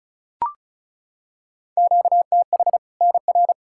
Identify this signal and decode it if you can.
Morse code, heard as a keyed tone, about 36 wpm, 700 Hz. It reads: QTH NR
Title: Practice QSO - Exchange 1 - Beginning - 35wpm